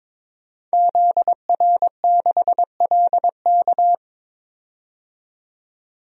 Morse code, heard as a keyed tone, about 21 wpm, 700 Hz. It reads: ZR6LK
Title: DXCC-100 Call Signs - 5 Characters - 22wpm